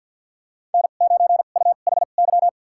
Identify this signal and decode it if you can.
Morse code, heard as a keyed tone, about 53 wpm, spaced as for 37 wpm, 700 Hz. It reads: N9UHY
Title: USA Call Signs - 5 Characters - 2x Character Spacing - 50wpm